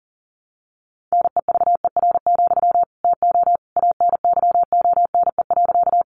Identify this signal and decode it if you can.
Morse code, heard as a keyed tone, about 37 wpm, 700 Hz. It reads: NEVER, TO ANYONE.
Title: Sentences from Top 900 Words 40wpm